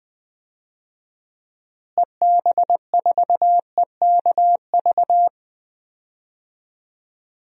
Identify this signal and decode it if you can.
Morse code, heard as a keyed tone, about 21 wpm, 700 Hz. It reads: EB4EKV
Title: DXCC-100 Call Signs - 6 Characters - Repeated 3x - 20wpm